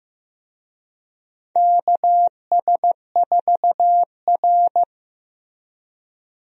Morse code, heard as a keyed tone, about 15 wpm, 700 Hz. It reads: KS4R
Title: USA Call Signs - 4 Characters - Repeated 3x - 15wpm